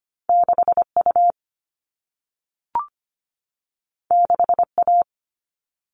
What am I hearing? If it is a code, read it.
Morse code, heard as a keyed tone, about 25 wpm, 700 Hz. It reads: 6U 6A
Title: All Cut Numbers - 2-Digit 25wpm